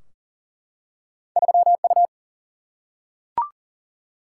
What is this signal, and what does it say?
Morse code, heard as a keyed tone, about 40 wpm, 700 Hz. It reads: 3U
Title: Single-Letter Single-Number 40wpm